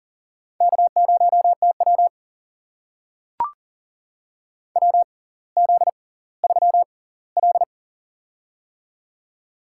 Morse code, heard as a keyed tone, about 40 wpm, spaced as for 20 wpm, 700 Hz. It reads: K0TW WZ3L
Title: USA Call Signs - 4 Characters - 6x Character Spacing - 40wpm